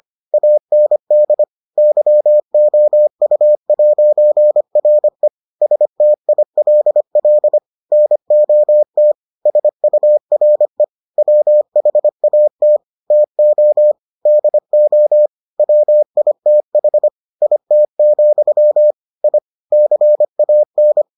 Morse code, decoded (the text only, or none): AND YOU'RE STILL NOT SURE WHAT TO DO WITH IT, I CAN